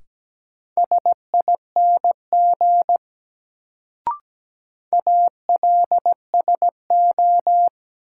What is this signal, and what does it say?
Morse code, heard as a keyed tone, about 17 wpm, 700 Hz. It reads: SING ALSO
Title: Sentences from Top 400 Words - New Word Drill - 17wpm